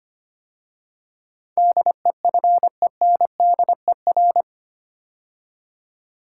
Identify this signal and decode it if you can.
Morse code, heard as a keyed tone, about 25 wpm, 690 Hz. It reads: DEFENDER
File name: 8 Letter Words - First 3 Letters Given - 25wpm